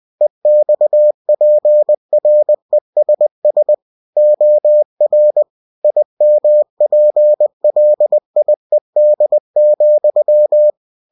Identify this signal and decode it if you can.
Morse code, heard as a keyed tone, about 20 wpm, 600 Hz. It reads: EXPRESS OR IMPLIED,
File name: cw-books / winnie-the-pooh / winnie-20wpm-page-0119